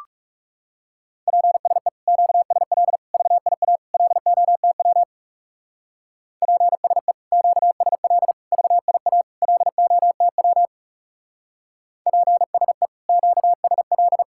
Morse code, heard as a keyed tone, about 37 wpm, 700 Hz. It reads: PSE QSL VIA LOTW PSE QSL VIA LOTW PSE QSL
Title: Speed-Racing Common QSO Phrases - 30wpm